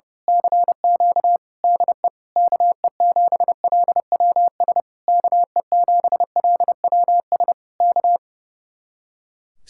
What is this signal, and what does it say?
Morse code, heard as a keyed tone, about 30 wpm, 700 Hz. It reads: CQ DE KE7LWS KE7LWS K
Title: Speed-Racing Common QSO Phrases - 30wpm